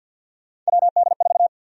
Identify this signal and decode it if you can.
Morse code, heard as a keyed tone, about 48 wpm, 700 Hz. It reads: WD4
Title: USA Call Sign Prefixes - 50wpm